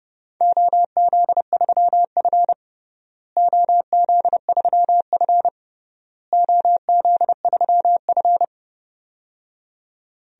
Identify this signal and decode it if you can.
Morse code, heard as a keyed tone, about 29 wpm, 700 Hz. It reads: OZ3F OZ3F OZ3F
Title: DXCC-100 Call Signs - 4 Characters - Repeated 3x - 30wpm